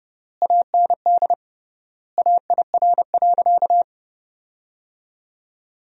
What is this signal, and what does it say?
Morse code, heard as a keyed tone, about 30 wpm, 700 Hz. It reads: AND AIR.
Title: 3x Word-Spacing - Sentences from Top 400 Words - 30wpm